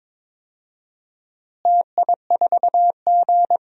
Morse code, heard as a keyed tone, about 22 wpm, 700 Hz. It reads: TI4G
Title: DXCC-100 Call Signs - 4 Characters - Repeated 3x - 22wpm